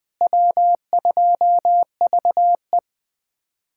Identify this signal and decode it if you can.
Morse code, heard as a keyed tone, about 20 wpm, 700 Hz. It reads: W2VE
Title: USA Callsign DE Callsign - 4 Characters - 20wpm